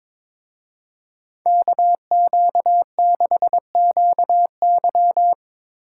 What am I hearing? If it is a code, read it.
Morse code, heard as a keyed tone, about 25 wpm, 700 Hz. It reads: KQ6QY
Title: Speed Racing - USA 2x2 Call Signs 22wpm